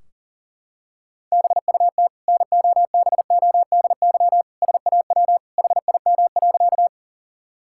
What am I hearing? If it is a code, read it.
Morse code, heard as a keyed tone, about 37 wpm, 700 Hz. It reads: BUT NOBODY SAW HIM.
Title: Sentences from Top 1600 Words 40wpm